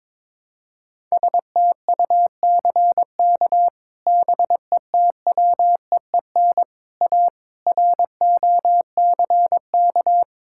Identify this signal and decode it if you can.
Morse code, heard as a keyed tone, about 22 wpm, 700 Hz. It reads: STUCK BETWEEN A ROCK
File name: practice / 22wpm / Sets of 8 Words - 22wpm